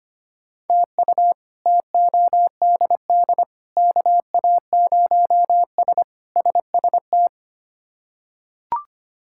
Morse code, heard as a keyed tone, about 25 wpm, 700 Hz. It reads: TU TODD KA0S SST
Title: K1USN SST QSO Practice - 25wpm